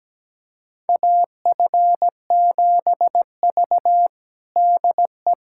Morse code, heard as a keyed tone, about 17 wpm, 700 Hz. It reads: AF7V DE
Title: USA Callsign DE Callsign - 4 Characters - 17wpm